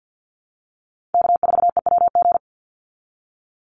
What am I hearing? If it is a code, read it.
Morse code, heard as a keyed tone, about 49 wpm, 700 Hz. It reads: K4EWG
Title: Call Signs Repeated 3x - 50wpm